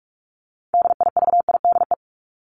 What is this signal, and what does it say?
Morse code, heard as a keyed tone, about 44 wpm, 700 Hz. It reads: DIVIDE
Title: Top 1501-1750 Words 45wpm